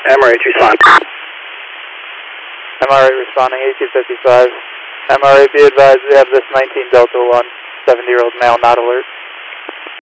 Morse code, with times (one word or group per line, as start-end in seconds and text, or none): none